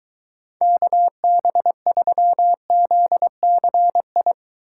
Morse code, heard as a keyed tone, about 23 wpm, 700 Hz. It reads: KB3ZCI